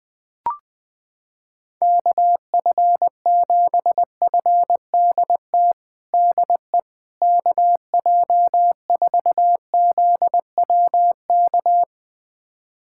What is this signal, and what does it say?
Morse code, heard as a keyed tone, about 20 wpm, 700 Hz. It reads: KF7FDT DE KJ4ZWK
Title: USA Callsign DE Callsign - 6 Characters - 20wpm